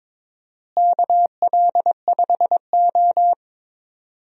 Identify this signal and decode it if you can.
Morse code, heard as a keyed tone, about 22 wpm, 700 Hz. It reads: KL5O
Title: USA Call Signs - 4 Characters - Repeated 3x - 22wpm